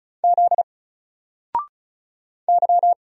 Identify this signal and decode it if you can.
Morse code, heard as a keyed tone, about 36 wpm, 700 Hz. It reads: Z Y
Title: Single Letter-Number 35wpm